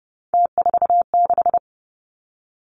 Morse code, heard as a keyed tone, about 31 wpm, 700 Hz. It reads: T46
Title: Three Challenging Letter-Number Combinations 30wpm